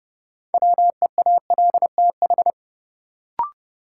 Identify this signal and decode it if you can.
Morse code, heard as a keyed tone, about 30 wpm, 700 Hz. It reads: WEALTH
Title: Sentences from Top 1000 Words - Review Part 4 - New Word Drill - 30wpm